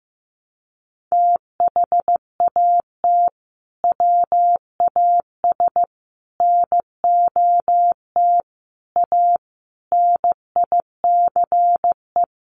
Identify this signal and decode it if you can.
Morse code, heard as a keyed tone, about 15 wpm, 700 Hz. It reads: THAT WAS NOT A NICE
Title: Sentences from Top 1200 Words 15wpm